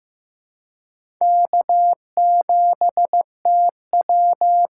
Morse code, heard as a keyed tone, about 15 wpm, 700 Hz. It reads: K7TW
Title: USA Call Signs - 4 Characters - 4x Character Spacing - 15wpm